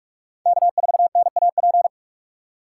Morse code, heard as a keyed tone, about 44 wpm, 700 Hz. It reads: K4NAP